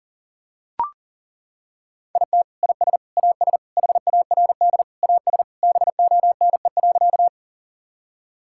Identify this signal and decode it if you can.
Morse code, heard as a keyed tone, about 40 wpm, 700 Hz. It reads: IT IS AS HARD AS BONE.
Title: Sentences from Top 1600 Words 40wpm